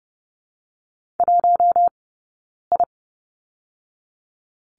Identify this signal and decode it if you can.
Morse code, heard as a keyed tone, about 30 wpm, 700 Hz. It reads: 1 I
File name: Lesson 20d - Number 3 - Two Characters 30wpm